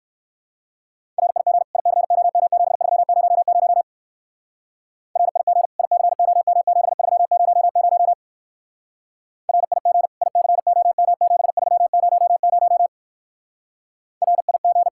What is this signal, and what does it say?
Morse code, heard as a keyed tone, about 58 wpm, 700 Hz. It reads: RIG ICOM7300 RIG ICOM7300 RIG ICOM7300 RIG